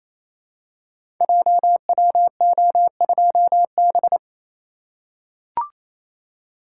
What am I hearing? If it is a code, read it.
Morse code, heard as a keyed tone, about 28 wpm, 700 Hz. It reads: JWO2B
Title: Code Groups - 5 Letter:Numbers - 28wpm